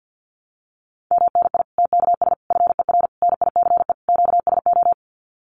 Morse code, heard as a keyed tone, about 48 wpm, 700 Hz. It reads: MNI TKS FER NICE QSO
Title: Common QSO Phrases 50wpm